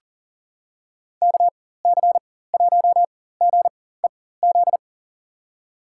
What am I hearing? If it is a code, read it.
Morse code, heard as a keyed tone, about 40 wpm, spaced as for 19 wpm, 700 Hz. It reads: KC1GEZ
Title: USA Call Signs - 4x Character Spacing - 40wpm